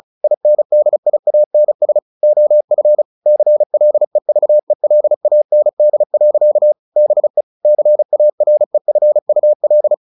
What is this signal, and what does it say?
Morse code, heard as a keyed tone, about 34 wpm, 600 Hz. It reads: INDIANS OF CLEVELAND. BE CAREFUL
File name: the old man and the sea / the old man and the sea 1 35wpm